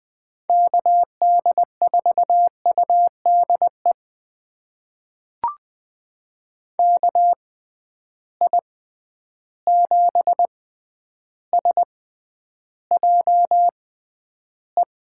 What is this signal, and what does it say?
Morse code, heard as a keyed tone, about 21 wpm, spaced as for 10 wpm, 700 Hz. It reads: KD4UDE KI7SJE
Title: USA Call Signs - 6 Characters - 6x Character Spacing - 20wpm